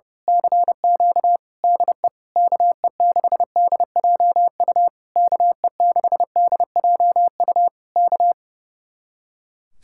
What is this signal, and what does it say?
Morse code, heard as a keyed tone, about 28 wpm, 700 Hz. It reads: CQ DE KE6DJU KE6DJU K
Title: Speed-Racing Common QSO Phrases - 30wpm